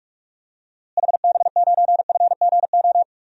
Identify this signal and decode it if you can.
Morse code, heard as a keyed tone, about 43 wpm, 700 Hz. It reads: HB9FGO
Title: DXCC-100 Call Signs - 6 Characters - Repeated 2x - 45wpm